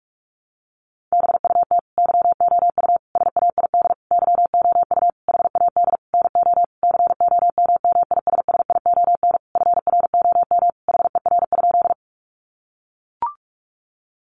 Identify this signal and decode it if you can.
Morse code, heard as a keyed tone, about 44 wpm, 700 Hz. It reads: BUT YOU SAID YOU HAD NO COMMISSION FROM HER?
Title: Sentences from Top 1400 Words 45wpm